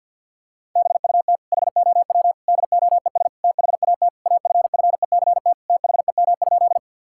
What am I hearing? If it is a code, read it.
Morse code, heard as a keyed tone, about 50 wpm, 700 Hz. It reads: BUT HOW DOES THAT AFFECT THEM?
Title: Sentences from Top 1000 Words - Review Part 6 - 50wpm